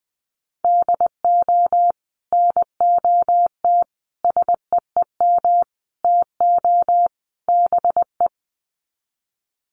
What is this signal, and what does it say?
Morse code, heard as a keyed tone, about 20 wpm, 700 Hz. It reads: DO NOT SEEM TO BE
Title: Sets of 5 Words Encore2 20wpm